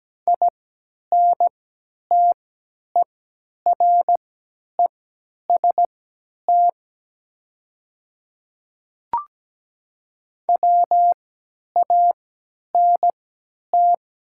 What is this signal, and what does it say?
Morse code, heard as a keyed tone, about 17 wpm, spaced as for 10 wpm, 690 Hz. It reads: INTEREST WANT
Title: 3x Character Spacing - Top 500 Words - 17wpm